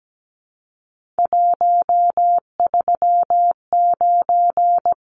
Morse code, heard as a keyed tone, about 17 wpm, 700 Hz. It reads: 139
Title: Numbers - 3-Digit 17wpm